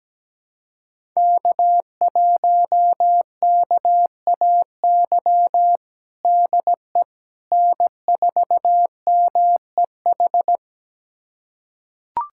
Morse code, heard as a keyed tone, about 17 wpm, 700 Hz. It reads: K1KAY DE N4MEH
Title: USA Callsign DE Callsign - 5 Characters - 17wpm